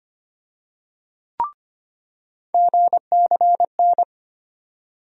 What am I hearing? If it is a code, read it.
Morse code, heard as a keyed tone, about 25 wpm, 700 Hz. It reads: GCN
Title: USA Call Sign Postfixes - 3 Letters 25wpm